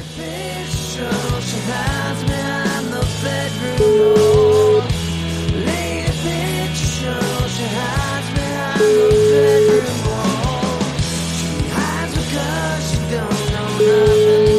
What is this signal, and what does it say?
Ringback tone (Europe, 425 Hz), with noise.